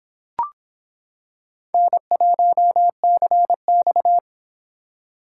Morse code, heard as a keyed tone, about 25 wpm, 700 Hz. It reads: N1CX